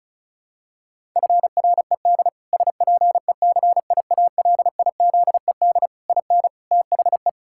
Morse code, heard as a keyed tone, about 34 wpm, 700 Hz. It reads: FRED SPECIALIZED IN THE
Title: Pangram Challenge v2 - 35wpm